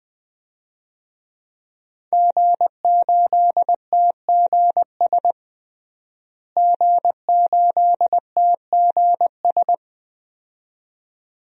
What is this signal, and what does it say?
Morse code, heard as a keyed tone, about 20 wpm, 700 Hz. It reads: G8TGS G8TGS
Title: DXCC-100 Call Signs - 5 Characters - Repeated 2x - 20wpm